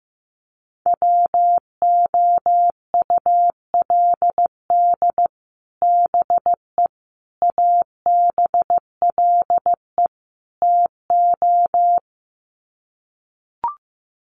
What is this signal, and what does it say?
Morse code, heard as a keyed tone, about 15 wpm, 700 Hz. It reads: WOULD BE ABLE TO
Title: Sets of 4 Words 15wpm